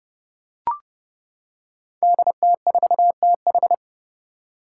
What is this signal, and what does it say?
Morse code, heard as a keyed tone, about 30 wpm, 700 Hz. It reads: DT4TH
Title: Five Challenging Letter-Number Combinations 30wpm